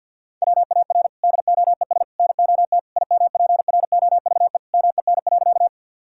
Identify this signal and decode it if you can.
Morse code, heard as a keyed tone, about 47 wpm, 690 Hz. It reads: WAR DOES NOT IMPROVE MEN.
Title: Sentences from Top 1000 Words - Review Part 5 - 50wpm